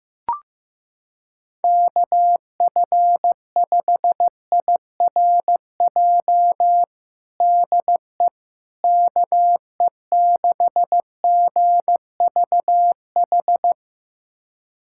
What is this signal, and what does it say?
Morse code, heard as a keyed tone, about 15 wpm, 700 Hz. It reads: KF5IRJ DE KE6GVH
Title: USA Callsign DE Callsign - 6 Characters - 15wpm